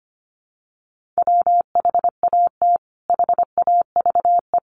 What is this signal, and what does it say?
Morse code, heard as a keyed tone, about 25 wpm, 700 Hz. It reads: WHAT HAVE
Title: Sentences from Top 400 Words 25wpm